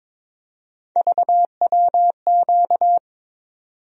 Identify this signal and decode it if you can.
Morse code, heard as a keyed tone, about 23 wpm, 700 Hz. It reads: VWQ
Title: USA Call Sign Postfixes - 3 Letters 22wpm